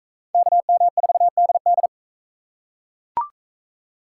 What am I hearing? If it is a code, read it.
Morse code, heard as a keyed tone, about 39 wpm, 700 Hz. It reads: KM4DD